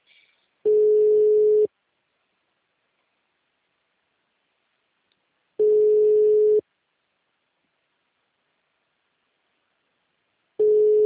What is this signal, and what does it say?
Ringback tone (Europe, 425 Hz).